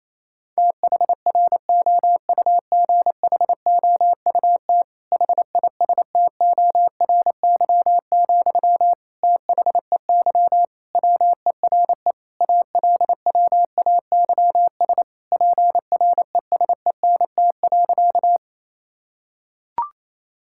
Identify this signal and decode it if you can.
Morse code, heard as a keyed tone, about 27 wpm, 700 Hz. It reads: THROUGHOUT HISTORY, THEY WERE ALWAYS PRESENT.
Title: Sentences from Top 1000 Words - Review Part 5 - 28wpm